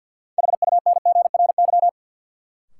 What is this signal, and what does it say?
Morse code, heard as a keyed tone, about 52 wpm, 700 Hz. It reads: HUNGRY
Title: Sentences from Top 1400 Words - New Word Drill - 50wpm